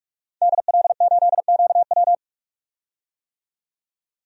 Morse code, heard as a keyed tone, about 45 wpm, 700 Hz. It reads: DL8QW
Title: DXCC-100 Call Signs - 5 Characters - 45wpm